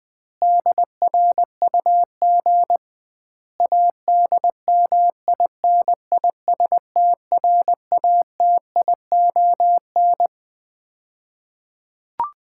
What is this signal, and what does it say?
Morse code, heard as a keyed tone, about 20 wpm, 700 Hz. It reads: DRUG ADMINISTRATION
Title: 2x Word-Spacing - Sets of 5 Words - 20wpm